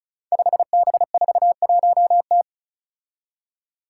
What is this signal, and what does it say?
Morse code, heard as a keyed tone, about 34 wpm, 700 Hz. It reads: 5B41T
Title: Five Challenging Letter-Number Combinations 35wpm